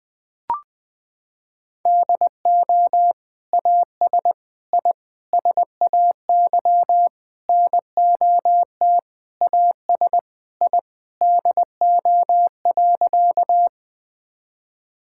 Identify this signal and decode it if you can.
Morse code, heard as a keyed tone, about 21 wpm, 700 Hz. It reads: DO AS I SAY NOT AS I DO.
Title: Sentences with 8-Word Phrases - 20wpm